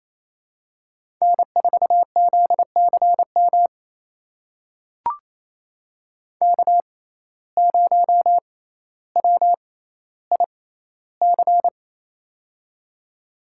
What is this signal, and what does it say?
Morse code, heard as a keyed tone, about 28 wpm, spaced as for 14 wpm, 700 Hz. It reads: N4ZCM K0WIC
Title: USA Call Signs - 5 Characters - 6x Character Spacing - 28wpm